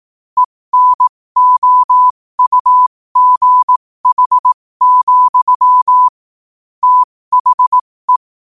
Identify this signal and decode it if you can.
Morse code, heard as a keyed tone, about 18 wpm, spaced as for 15 wpm, 1000 Hz. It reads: ENOUGH, THE
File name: qotdfiles / 15wpm